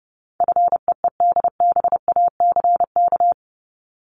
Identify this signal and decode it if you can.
Morse code, heard as a keyed tone, about 30 wpm, 700 Hz. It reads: FEEDBACK
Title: Top 1501-1750 Words 30wpm